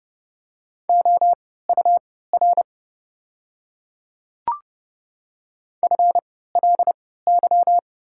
Morse code, heard as a keyed tone, about 32 wpm, spaced as for 17 wpm, 700 Hz. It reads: OUR FLY
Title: 3x Character Spacing - Top 500 Words - 30wpm